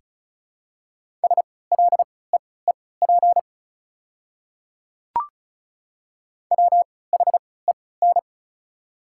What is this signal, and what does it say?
Morse code, heard as a keyed tone, about 36 wpm, spaced as for 20 wpm, 690 Hz. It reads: SLEEP WHEN
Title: 3x Character Spacing - Top 500 Words - 35wpm